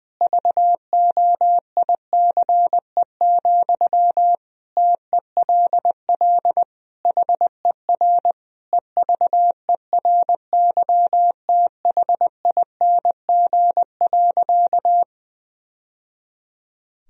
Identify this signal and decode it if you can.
Morse code, heard as a keyed tone, about 21 wpm, 700 Hz. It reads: VOICE, TELL HER EVERYTHING.
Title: Sentences from Top 1000 Words - Review Part 1 - 20wpm